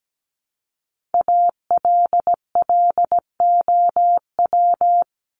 Morse code, heard as a keyed tone, about 17 wpm, 700 Hz. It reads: ALLOW
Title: Top 501-750 Words 17wpm